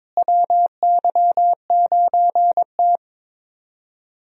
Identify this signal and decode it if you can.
Morse code, heard as a keyed tone, about 22 wpm, 700 Hz. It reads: WY9T
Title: USA Call Signs - 4 Characters - Repeated 3x - 22wpm